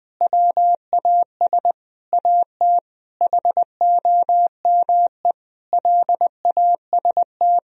Morse code, heard as a keyed tone, about 20 wpm, 700 Hz. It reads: WAS AT HOME LAST
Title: Sentences from Top 1000 Words - Review Part 2 - 20wpm